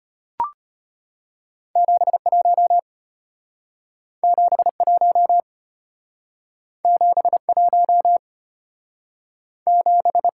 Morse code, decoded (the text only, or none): 71 71 71 7